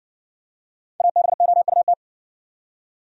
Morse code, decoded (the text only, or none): ABOUT